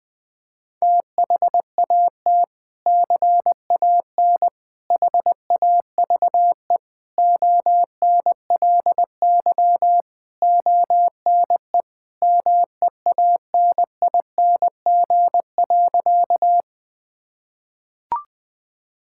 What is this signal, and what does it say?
Morse code, heard as a keyed tone, about 21 wpm, 700 Hz. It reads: THAT CAN HAVE ONLY ONE MEANING.
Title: Sentences from Top 1000 Words - Review Part 6 - 20wpm